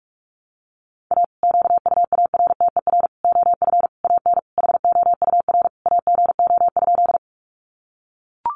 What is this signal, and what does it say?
Morse code, heard as a keyed tone, about 44 wpm, 700 Hz. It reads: A QUARTER OF AN HOUR AGO?
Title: Sentences from Top 1300 Words 45wpm